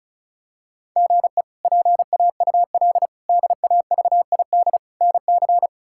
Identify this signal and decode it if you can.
Morse code, heard as a keyed tone, about 36 wpm, 700 Hz. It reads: GE PAUL DAVID NC
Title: K1USN SST QSO Practice - 35wpm